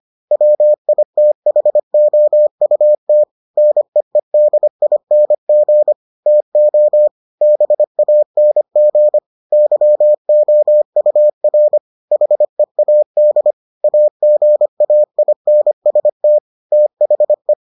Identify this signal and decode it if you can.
Morse code, heard as a keyed tone, about 25 wpm, 600 Hz. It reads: WITHOUT NEEDING TO BANG YOUR HEAD AGAINST THE